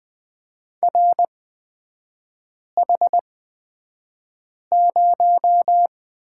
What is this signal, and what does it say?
Morse code, heard as a keyed tone, about 20 wpm, spaced as for 5 wpm, 700 Hz. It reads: RH0
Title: 243 Lesson 39e - Letter z - Three Characters 20wpm